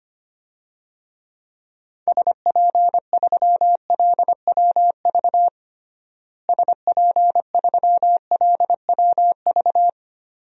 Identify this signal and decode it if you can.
Morse code, heard as a keyed tone, about 25 wpm, 700 Hz. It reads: SP3LWV SP3LWV
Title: DXCC-100 Call Signs - 6 Characters - Repeated 2x - 25wpm